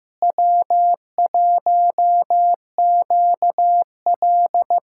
Morse code, heard as a keyed tone, about 16 wpm, 700 Hz. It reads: W1QL
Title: USA Callsign DE Callsign - 4 Characters - 15wpm